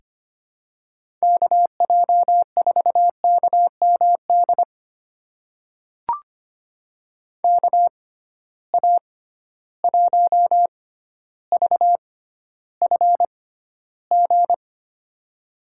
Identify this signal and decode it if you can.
Morse code, heard as a keyed tone, about 25 wpm, spaced as for 13 wpm, 700 Hz. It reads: KJ4KMD KA1VFG
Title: USA Call Signs - 6 Characters - 6x Character Spacing - 25wpm